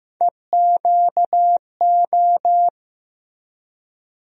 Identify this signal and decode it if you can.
Morse code, heard as a keyed tone, about 15 wpm, 700 Hz. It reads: EQO
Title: USA Call Sign Postfixes - 3 Letters 15wpm